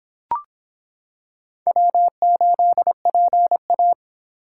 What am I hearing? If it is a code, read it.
Morse code, heard as a keyed tone, about 25 wpm, 700 Hz. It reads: W8PA